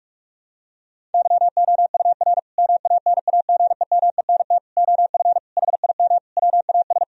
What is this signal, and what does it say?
Morse code, heard as a keyed tone, about 42 wpm, 700 Hz. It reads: YOUR MANAGEMENT OF HIM WAS
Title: Sentences from Top 1000 Words - Review Part 4 - 45wpm